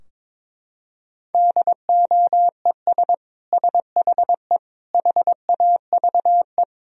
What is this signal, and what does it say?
Morse code, heard as a keyed tone, about 22 wpm, 700 Hz. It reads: DOES SHE HAVE
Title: Sentences from Top 1000 Words - Review Part 3 - 22wpm